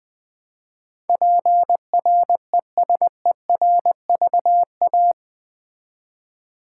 Morse code, heard as a keyed tone, about 20 wpm, 700 Hz. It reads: PRESERVA
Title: Build A Word - 4 Letters At A Time - 12-Letter Words - 20wpm